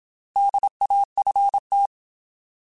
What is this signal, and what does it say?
Morse code, heard as a keyed tone, about 27 wpm, 800 Hz. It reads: DAFT
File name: Alphabetical / Alphabetical mp3s / Ds @25